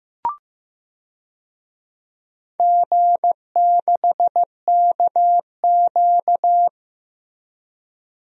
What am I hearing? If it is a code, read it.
Morse code, heard as a keyed tone, about 15 wpm, 700 Hz. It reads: G6KQ
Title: DXCC-100 Call Signs - 4 Characters - Repeated 3x - 15wpm